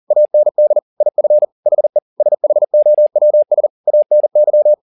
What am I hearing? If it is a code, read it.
Morse code, heard as a keyed tone, about 40 wpm, 600 Hz. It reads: AND IF HE SHOWS ANY
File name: the old man and the sea 2 40wpm